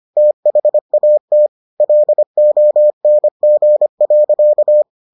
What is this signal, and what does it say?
Morse code, heard as a keyed tone, about 25 wpm, 600 Hz. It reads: THAT LONG.